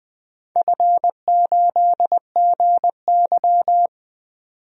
Morse code, heard as a keyed tone, about 20 wpm, 700 Hz. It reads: F8GY
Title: DXCC-100 Call Signs - 4 Characters - 20wpm